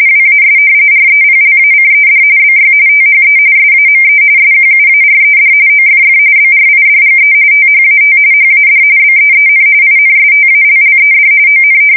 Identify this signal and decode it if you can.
RTTY, about 45.45 baud. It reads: WAS REPLACED BY THE SAME SQUARED OFF BRIDGE THAT HAD BEEN INSTALLED ON
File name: fsk170 / battleshipnj